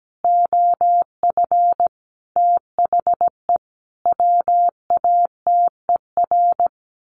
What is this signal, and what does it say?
Morse code, heard as a keyed tone, about 17 wpm, 700 Hz. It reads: OF THE WATER
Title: Sentences from Top 1300 Words 17wpm